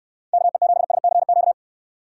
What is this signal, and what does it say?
Morse code, heard as a keyed tone, about 69 wpm, 700 Hz. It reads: KE7SCQ